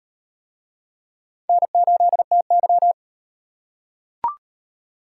Morse code, heard as a keyed tone, about 33 wpm, 700 Hz. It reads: N8TY